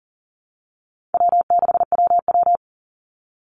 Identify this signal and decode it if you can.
Morse code, heard as a keyed tone, about 40 wpm, 700 Hz. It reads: W6WW
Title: NA SOTA Chaser Callsigns 40wpm